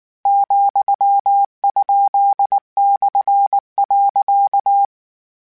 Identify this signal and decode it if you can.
Morse code, heard as a keyed tone, about 19 wpm, 800 Hz. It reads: ,?/.
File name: Misc / Misc-mp3s / Punct @18